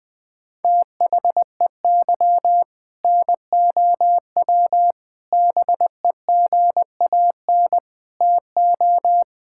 Morse code, heard as a keyed tone, about 20 wpm, 700 Hz. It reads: THEY NOW BEGAN TO
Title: Sentences from Top 1000 Words - Review Part 5 - 20wpm